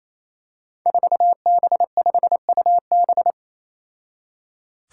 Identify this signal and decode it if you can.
Morse code, heard as a keyed tone, about 28 wpm, 700 Hz. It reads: 4B5UB
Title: Five Challenging Letter-Number Combinations 28wpm